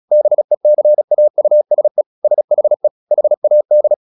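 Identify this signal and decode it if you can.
Morse code, heard as a keyed tone, about 36 wpm, 600 Hz. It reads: BECAUSE SHE HAD
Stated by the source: Morse practice at 36 wpm, text of Ken Follett's novel Never, from CW med Gnister.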